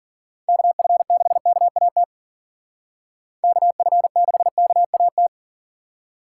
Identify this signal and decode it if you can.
Morse code, heard as a keyed tone, about 45 wpm, 700 Hz. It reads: KF6KAT KF6KAT
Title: Speed Racing - USA 2x3 Call Signs 40wpm